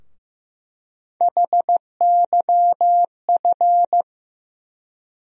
Morse code, heard as a keyed tone, about 15 wpm, 700 Hz. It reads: HYF
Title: USA Call Sign Postfixes - 3 Letters 15wpm